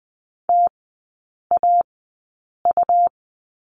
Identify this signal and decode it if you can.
Morse code, heard as a keyed tone, about 20 wpm, spaced as for 8 wpm, 700 Hz. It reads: TAU
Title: 152 Lesson 25e - Letter p - Three Characters 20wpm